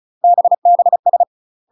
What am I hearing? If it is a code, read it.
Morse code, heard as a keyed tone, about 36 wpm, 700 Hz. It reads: BBS
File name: UK MASTER QSOs TRAINING / 35WPM / UK MASTER QSOs 0010